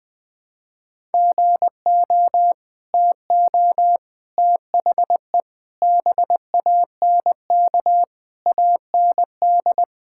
Morse code, heard as a keyed tone, about 21 wpm, 700 Hz. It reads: GO TO THE BANK AND
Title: Sentences from Top 1000 Words - Review Part 5 - 20wpm